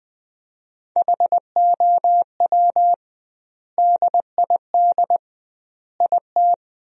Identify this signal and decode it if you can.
Morse code, heard as a keyed tone, about 20 wpm, 700 Hz. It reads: HOW DID IT
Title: 2x Word-Spacing - Sentences from Top 300 Words - 20wpm